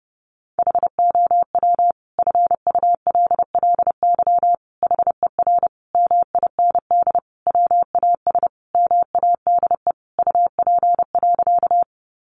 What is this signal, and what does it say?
Morse code, heard as a keyed tone, about 30 wpm, 700 Hz. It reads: HOW FULLY HER MIND WAS MADE UP.
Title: Sentences from Top 1300 Words 30wpm